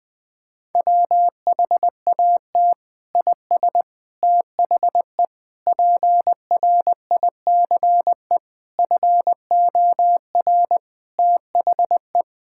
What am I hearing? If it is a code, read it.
Morse code, heard as a keyed tone, about 20 wpm, 700 Hz. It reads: WHAT IS THE PRICE FOR THE
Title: Sentences from Top 1000 Words - Review Part 6 - 20wpm